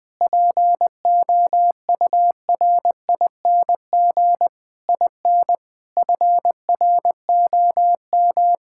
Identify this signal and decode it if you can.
Morse code, heard as a keyed tone, about 20 wpm, 700 Hz. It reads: POURING IN FROM
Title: Sentences with 9-Word Phrases - 20wpm